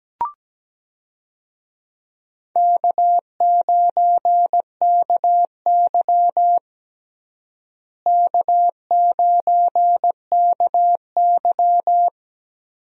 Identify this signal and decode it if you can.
Morse code, heard as a keyed tone, about 17 wpm, 700 Hz. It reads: K9KY K9KY
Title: USA Call Signs - 4 Characters - Repeated 2x - 17wpm